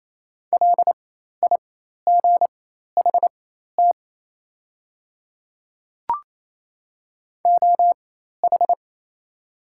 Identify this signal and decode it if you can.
Morse code, heard as a keyed tone, about 28 wpm, spaced as for 13 wpm, 700 Hz. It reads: LIGHT OH
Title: 4x Character Spacing - Top 500 Words - 28wpm